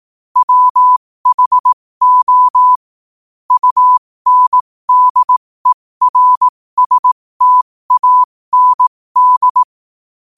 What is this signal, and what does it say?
Morse code, heard as a keyed tone, about 18 wpm, spaced as for 15 wpm, 1000 Hz. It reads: WHO UNDERSTAND
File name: qotdfiles / 15wpm